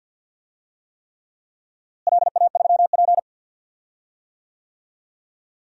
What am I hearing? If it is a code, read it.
Morse code, heard as a keyed tone, about 52 wpm, 700 Hz. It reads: LA3P
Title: DXCC-100 Call Signs - 4 Characters - Repeated 3x - 50wpm